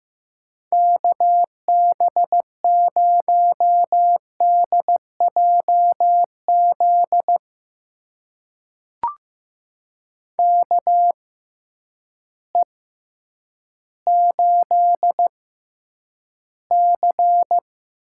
Morse code, heard as a keyed tone, about 15 wpm, spaced as for 9 wpm, 700 Hz. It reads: KB0DJZ KE8C
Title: USA Call Signs - 6 Characters - 6x Character Spacing - 15wpm